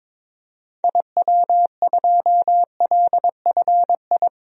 Morse code, heard as a keyed tone, about 21 wpm, 700 Hz. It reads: IW2LFI
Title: DXCC-100 Call Signs - 6 Characters - 22wpm